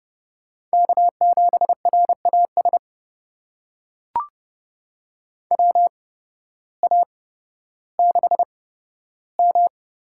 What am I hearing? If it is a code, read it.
Morse code, heard as a keyed tone, about 31 wpm, spaced as for 14 wpm, 700 Hz. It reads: K7RAS WA6M
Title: USA Call Signs - 8x Character Spacing - 30wpm